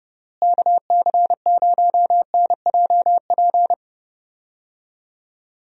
Morse code, heard as a keyed tone, about 30 wpm, 700 Hz. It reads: KC0NJP
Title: USA Call Signs - 6 Characters - Repeated 2x - 30wpm